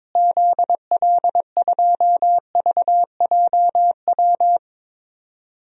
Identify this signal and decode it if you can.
Morse code, heard as a keyed tone, about 22 wpm, 700 Hz. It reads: ZL2VJW
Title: DXCC-100 Call Signs - 6 Characters - Repeated 3x - 22wpm